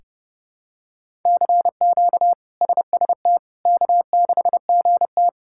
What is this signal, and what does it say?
Morse code, heard as a keyed tone, about 30 wpm, 700 Hz. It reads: CQ SST K6GT
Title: K1USN SST QSO Practice - 30wpm